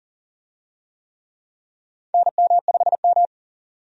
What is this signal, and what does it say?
Morse code, heard as a keyed tone, about 40 wpm, 700 Hz. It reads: NM5M
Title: USA Call Signs - 4 Characters - 40wpm